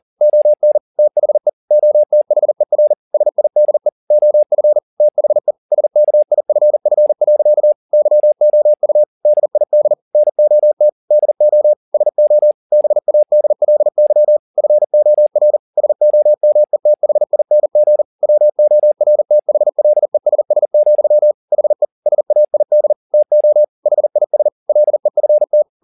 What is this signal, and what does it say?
Morse code, heard as a keyed tone, about 40 wpm, 600 Hz. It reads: ON THE OTHER SIDE OF THE SKIFF. YOU DID NOT DO SO BADLY FOR SOMETHING WORTHLESS, HE SAID TO HIS LEFT
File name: the old man and the sea 2 40wpm